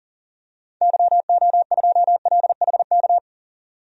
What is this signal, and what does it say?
Morse code, heard as a keyed tone, about 40 wpm, 700 Hz. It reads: YO2LHK
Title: DXCC-100 Call Signs - 6 Characters - 40wpm